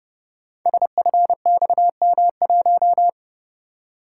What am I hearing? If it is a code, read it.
Morse code, heard as a keyed tone, about 30 wpm, 700 Hz. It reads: SFXM1
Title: Code Groups - 5 Letter:Numbers - 30wpm